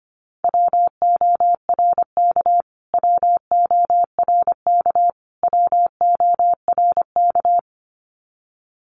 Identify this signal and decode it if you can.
Morse code, heard as a keyed tone, about 25 wpm, 700 Hz. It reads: WORK WORK WORK
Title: Top 200 Words Repeated 25wpm